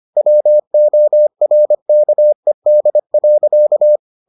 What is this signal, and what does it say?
Morse code, heard as a keyed tone, about 25 wpm, 600 Hz. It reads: WORKED.